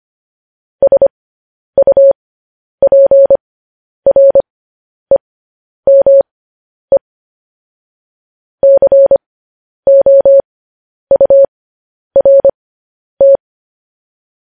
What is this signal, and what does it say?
Morse code, heard as a keyed tone, about 25 wpm, spaced as for 10 wpm, 550 Hz. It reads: SUPREME COURT